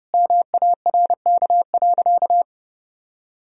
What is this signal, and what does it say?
Morse code, heard as a keyed tone, about 30 wpm, 700 Hz. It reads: MARK.
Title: Sentences from Top 1000 Words - Review Part 2 - 30wpm